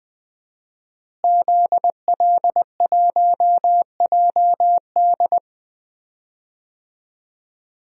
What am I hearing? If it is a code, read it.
Morse code, heard as a keyed tone, about 20 wpm, 700 Hz. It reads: ZL1JD
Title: DXCC-100 Call Signs - 5 Characters - Repeated 3x - 20wpm